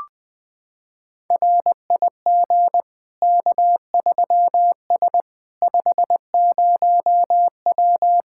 Morse code, heard as a keyed tone, about 20 wpm, 700 Hz. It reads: RIG K3S 50W
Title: Practice QSO - Exchange 2 - Info - 20wpm